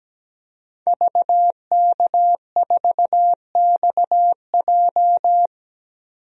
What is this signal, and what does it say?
Morse code, heard as a keyed tone, about 17 wpm, 700 Hz. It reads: VK4XJ
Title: DXCC-100 Call Signs - 5 Characters - 17wpm